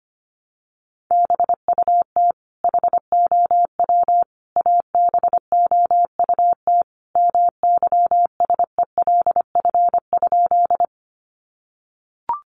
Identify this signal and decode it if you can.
Morse code, heard as a keyed tone, about 25 wpm, 700 Hz. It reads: BUT HOW ABOUT MYSELF?
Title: Sentences from Top 1300 Words 25wpm